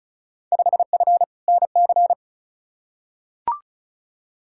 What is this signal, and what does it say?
Morse code, heard as a keyed tone, about 36 wpm, 690 Hz. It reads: 5F NC
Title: Field Day Exchanges Encore - 35wpm